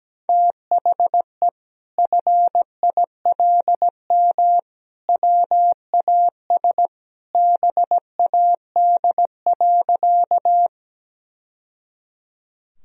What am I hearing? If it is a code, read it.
Morse code, heard as a keyed tone, about 17 wpm, 700 Hz. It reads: THE FILM WAS BAD.
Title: Sentences from Top 1000 Words - Review Part 3 - 17wpm